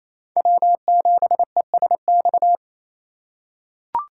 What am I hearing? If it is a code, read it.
Morse code, heard as a keyed tone, about 28 wpm, 700 Hz. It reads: W7ESX
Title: USA Call Signs - 8x Character Spacing - 28wpm